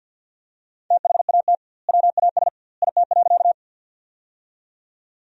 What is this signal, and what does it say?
Morse code, heard as a keyed tone, about 47 wpm, 700 Hz. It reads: THAT WAS IT.
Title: 2x Word-Spacing - Sentences from Top 100 Words - 50wpm